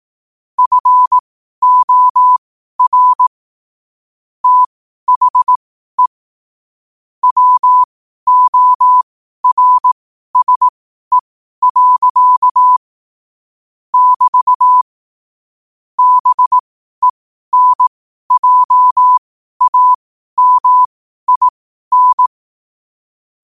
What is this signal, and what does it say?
Morse code, heard as a keyed tone, about 18 wpm, spaced as for 12 wpm, 1000 Hz. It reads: FOR THE WORSE. <BT> BENJAMIN
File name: qotdfiles / 12wpm